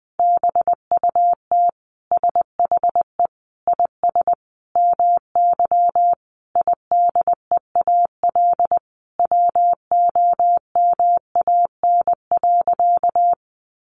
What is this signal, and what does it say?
Morse code, heard as a keyed tone, about 20 wpm, 700 Hz. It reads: BUT SHE IS MY IDEAL WOMAN.
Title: Sentences from Top 1400 Words 20wpm